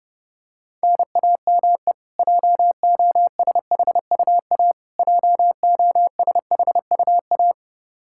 Morse code, heard as a keyed tone, about 30 wpm, 700 Hz. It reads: NAME JOSHUA JOSHUA
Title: Practice QSO - Exchange 1 - Beginning - 30wpm